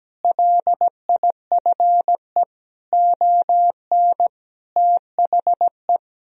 Morse code, heard as a keyed tone, about 17 wpm, 700 Hz. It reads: LIFE ON THE
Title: Sentences from Top 1000 Words - Review Part 3 - 17wpm